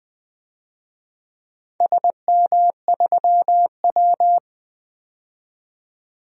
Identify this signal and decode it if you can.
Morse code, heard as a keyed tone, about 20 wpm, 700 Hz. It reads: SM3W
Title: DXCC-100 Call Signs - 4 Characters - 20wpm